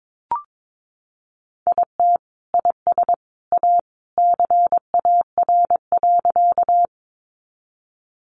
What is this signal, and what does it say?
Morse code, heard as a keyed tone, about 22 wpm, 700 Hz. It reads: IT IS A CAR.
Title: Easy Phrases 22wpm